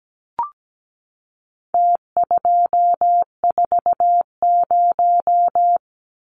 Morse code, heard as a keyed tone, about 17 wpm, 700 Hz. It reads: T240
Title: Numbers - Contest Style - 4-Digit 17wpm